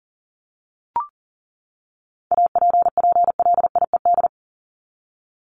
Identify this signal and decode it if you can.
Morse code, heard as a keyed tone, about 40 wpm, 700 Hz. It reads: APPLIED
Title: ED Suffix - Top 100 Words with Base Words 40wpm